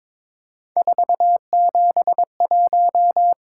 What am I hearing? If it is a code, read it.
Morse code, heard as a keyed tone, about 26 wpm, 700 Hz. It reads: 471
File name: Speed-Racing - Three Letter-Number Combinations - 22wpm